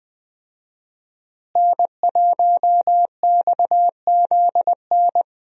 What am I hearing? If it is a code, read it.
Morse code, heard as a keyed tone, about 20 wpm, 700 Hz. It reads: N1XZN
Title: USA Call Signs - 5 Characters - Repeated 2x - 20wpm